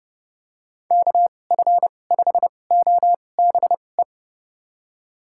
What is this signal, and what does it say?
Morse code, heard as a keyed tone, about 30 wpm, spaced as for 22 wpm, 700 Hz. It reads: KF5OBE
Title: USA Call Signs - 6 Characters - 2x Character Spacing - 30wpm